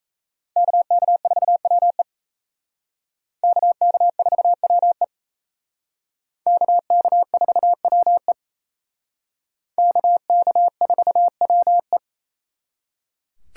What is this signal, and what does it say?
Morse code, heard as a keyed tone, about 35 wpm, 700 Hz. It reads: KK4WE KK4WE KK4WE KK4WE
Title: Speed Racing - USA 2x2 Call Signs 28wpm